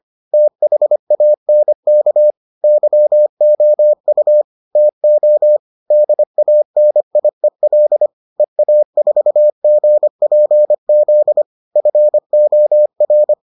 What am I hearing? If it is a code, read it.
Morse code, heard as a keyed tone, about 25 wpm, 600 Hz. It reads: THANK YOU TO DANIEL EA4GPZ FOR